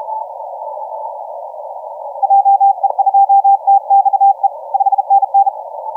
Morse code, heard as a keyed tone, about 34 wpm, 790 Hz, with radio noise. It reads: JE2TXE 5NN
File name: TM0SI 0722UTC 17m